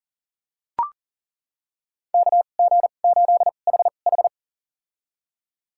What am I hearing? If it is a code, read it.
Morse code, heard as a keyed tone, about 40 wpm, spaced as for 29 wpm, 700 Hz. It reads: KG8HH
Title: USA Call Signs - 2x Character Spacing - 40wpm